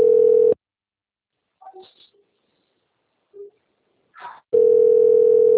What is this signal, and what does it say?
Ringback tone (North America), 440 + 480 Hz.
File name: out- - - - - .44968